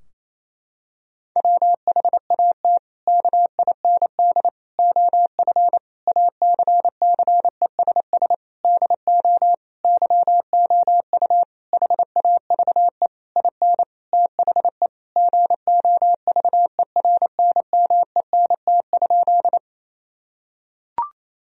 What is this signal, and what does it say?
Morse code, heard as a keyed tone, about 28 wpm, 700 Hz. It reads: WHAT KIND OF ACCESS DO YOU HAVE IN THE GOVERNMENT?
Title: Sentences from Top 1000 Words - Review Part 4 - 28wpm